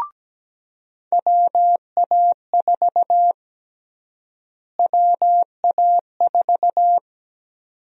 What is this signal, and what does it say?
Morse code, heard as a keyed tone, about 17 wpm, 700 Hz. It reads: WA4 WA4
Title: USA Call Sign Prefixes - Repeated 2x - 17wpm